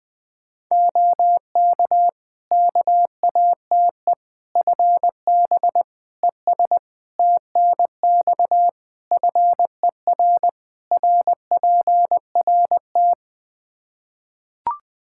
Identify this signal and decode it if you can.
Morse code, heard as a keyed tone, about 20 wpm, 700 Hz. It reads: OK KATE FB ES TNX FER RPRT
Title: Practice QSO - Exchange 2 - Info - 20wpm